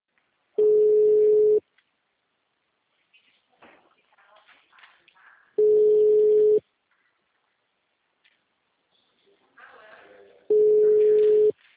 Ringback tone (Europe, 425 Hz).
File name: out- -2007- - - .214